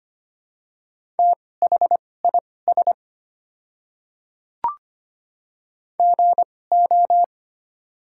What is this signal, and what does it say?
Morse code, heard as a keyed tone, about 25 wpm, spaced as for 18 wpm, 700 Hz. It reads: THIS GO
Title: 2x Character Spacing - Top 100 Words - 25wpm